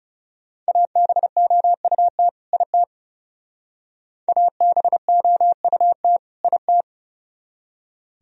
Morse code, heard as a keyed tone, about 39 wpm, 700 Hz. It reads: ABOUT IT ABOUT IT